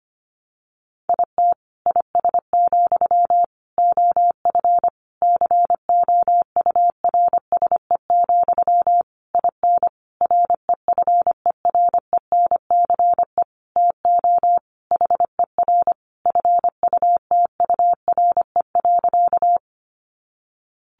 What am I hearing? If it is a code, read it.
Morse code, heard as a keyed tone, about 25 wpm, 700 Hz. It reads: IT IS, OF COURSE, IN REFERENCE TO HER FUTURE.
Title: Sentences from Top 1200 Words 25wpm